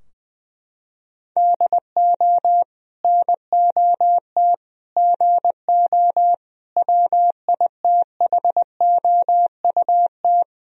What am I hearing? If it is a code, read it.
Morse code, heard as a keyed tone, about 21 wpm, 700 Hz. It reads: DO NOT GO WITHOUT
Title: Sentences from Top 1000 Words - Review Part 3 - 20wpm